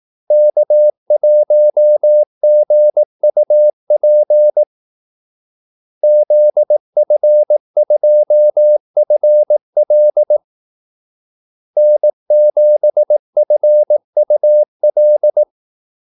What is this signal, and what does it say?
Morse code, heard as a keyed tone, about 18 wpm, 600 Hz. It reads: K1GUP ZF2FL N7FUL
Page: Kaldesignaler 18 wpm